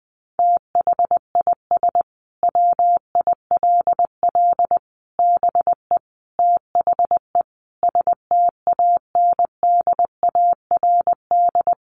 Morse code, heard as a keyed tone, about 20 wpm, 700 Hz. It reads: THIS WILL BE THE STANDARD
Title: Sentences from Top 1200 Words 20wpm